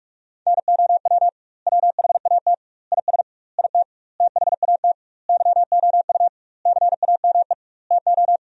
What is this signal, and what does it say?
Morse code, heard as a keyed tone, about 44 wpm, 700 Hz. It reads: NOW WHAT IS IT THAT YOU CAME TO
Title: 2x Word-Spacing - Sentences from Top 200 Words - 45wpm